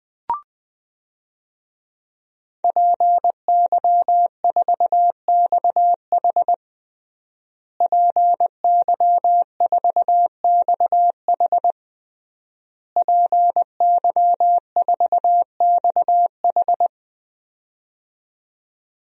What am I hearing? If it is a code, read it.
Morse code, heard as a keyed tone, about 20 wpm, 700 Hz. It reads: PY4XH PY4XH PY4XH
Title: DXCC-100 Call Signs - 5 Characters - Repeated 3x - 20wpm